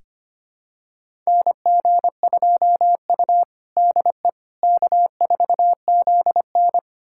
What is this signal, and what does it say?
Morse code, heard as a keyed tone, about 25 wpm, 700 Hz. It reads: NG2U DE K4ZN
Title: USA Callsign DE Callsign - 4 Characters - 25wpm